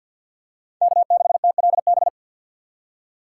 Morse code, heard as a keyed tone, about 53 wpm, 700 Hz. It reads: K6TLB
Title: USA Call Signs - 5 Characters - 2x Character Spacing - 50wpm